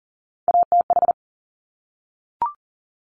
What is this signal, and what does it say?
Morse code, heard as a keyed tone, about 39 wpm, 700 Hz. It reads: ATH
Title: Top Three Letter Combinations in Words 40wpm